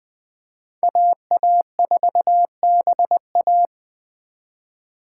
Morse code, heard as a keyed tone, about 20 wpm, 700 Hz. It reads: AA4BA
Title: Speed Racing - USA Call Signs 17wpm